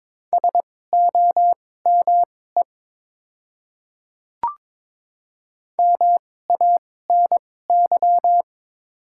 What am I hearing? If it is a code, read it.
Morse code, heard as a keyed tone, about 21 wpm, spaced as for 16 wpm, 690 Hz. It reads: SOME MANY
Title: 2x Character Spacing - Top 100 Words - 22wpm